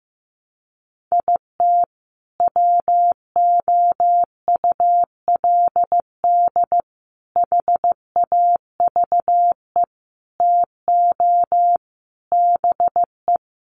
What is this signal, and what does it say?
Morse code, heard as a keyed tone, about 15 wpm, 700 Hz. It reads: IT WOULD HAVE TO BE
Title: Sets of 5 Words Encore 15wpm